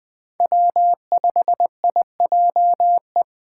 Morse code, heard as a keyed tone, about 21 wpm, 700 Hz. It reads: W5IJE
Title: Speed Racing - USA 1x3 Call Signs 17wpm